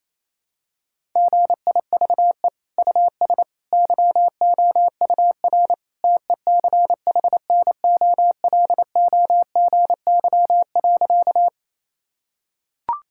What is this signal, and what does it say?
Morse code, heard as a keyed tone, about 28 wpm, 700 Hz. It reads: GIVE US YOUR TECHNOLOGY.
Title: Sentences from Top 1000 Words - Review Part 5 - 28wpm